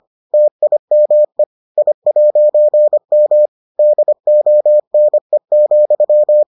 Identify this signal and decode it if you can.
Morse code, heard as a keyed tone, about 25 wpm, 600 Hz. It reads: TIME I'M DONE,